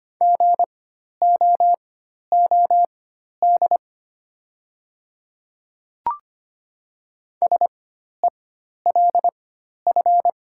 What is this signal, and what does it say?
Morse code, heard as a keyed tone, about 25 wpm, spaced as for 12 wpm, 700 Hz. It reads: GOOD SELF
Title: 4x Character Spacing - Top 300 Words - 25wpm